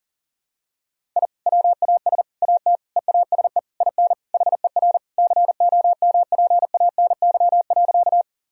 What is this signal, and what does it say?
Morse code, heard as a keyed tone, about 40 wpm, 700 Hz. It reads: I WAS AT EASE IN HER COMPANY.
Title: Sentences from Top 1000 Words - Review Part 1 - 40wpm